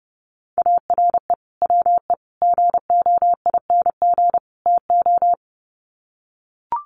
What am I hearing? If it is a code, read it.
Morse code, heard as a keyed tone, about 30 wpm, 700 Hz. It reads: ARE WE GOING TO
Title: Sets of 4 Words 30wpm